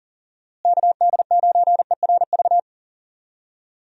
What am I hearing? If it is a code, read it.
Morse code, heard as a keyed tone, about 40 wpm, 700 Hz. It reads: KD9ERV
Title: USA Call Signs - 6 Characters - Repeated 3x - 40wpm